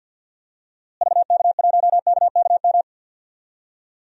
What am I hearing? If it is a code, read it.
Morse code, heard as a keyed tone, about 51 wpm, 700 Hz. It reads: VK1KKM
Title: DXCC-100 Call Signs - 6 Characters - Repeated 3x - 50wpm